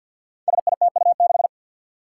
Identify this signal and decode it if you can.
Morse code, heard as a keyed tone, about 47 wpm, 700 Hz. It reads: SITU6
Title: Five Challenging Letter-Number Combinations 50wpm